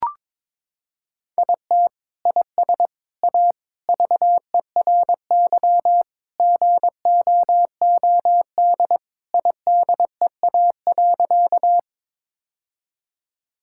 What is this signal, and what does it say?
Morse code, heard as a keyed tone, about 22 wpm, 700 Hz. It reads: IT IS A VERY GOOD IDEA.
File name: practice / 22wpm / Sentences with 3-Word Phrases - 22wpm